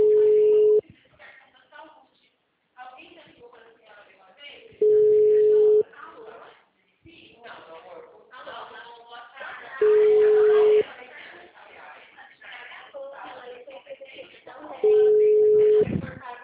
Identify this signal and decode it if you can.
Ringback tone (Europe, 425 Hz), with speech.
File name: out- -2007- - - .53499